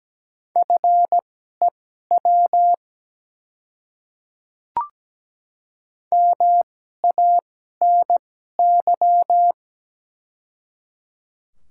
Morse code, heard as a keyed tone, about 17 wpm, spaced as for 12 wpm, 700 Hz. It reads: FEW MANY
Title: 2x Character Spacing - Top 300 Words - 17wpm